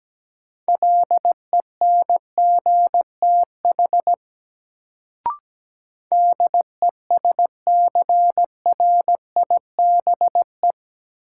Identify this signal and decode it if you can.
Morse code, heard as a keyed tone, about 17 wpm, 700 Hz. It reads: LENGTH DESCRIBE
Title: Sentences from Top 1000 Words - New Word Drill - 17wpm